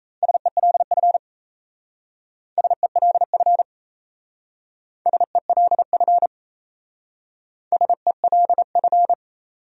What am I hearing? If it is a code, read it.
Morse code, heard as a keyed tone, about 37 wpm, 700 Hz. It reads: SELF SELF SELF SELF